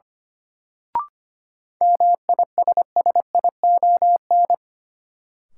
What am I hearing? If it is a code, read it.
Morse code, heard as a keyed tone, about 25 wpm, 700 Hz. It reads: MISSION
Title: Sentences from Top 1600 Words - New Word Drill Alt - 25wpm